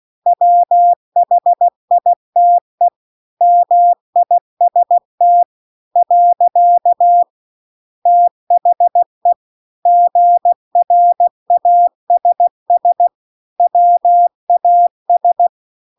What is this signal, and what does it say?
Morse code, heard as a keyed tone, about 16 wpm, 700 Hz. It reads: WHITE MIST . THE GRASS WAS